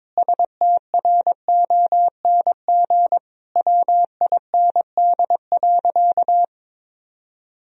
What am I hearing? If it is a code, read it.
Morse code, heard as a keyed tone, about 22 wpm, 700 Hz. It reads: STRONG WIND.
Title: Sentences from Top 1000 Words - Review Part 4 - 22wpm